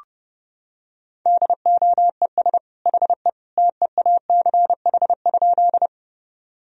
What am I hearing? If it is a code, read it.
Morse code, heard as a keyed tone, about 30 wpm, 700 Hz. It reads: DOES HE TEACH?
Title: Sentences from Top 1000 Words - Review Part 2 - 30wpm